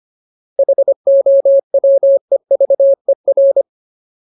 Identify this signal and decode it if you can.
Morse code, heard as a keyed tone, about 25 wpm, 550 Hz. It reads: HOWEVER
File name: Ultimate Word List / 25WPM Ultimate